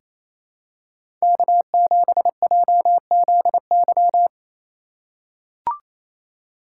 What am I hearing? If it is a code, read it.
Morse code, heard as a keyed tone, about 27 wpm, 700 Hz. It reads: K7JZY
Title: USA Call Signs - 5 Characters - 2x Character Spacing - 28wpm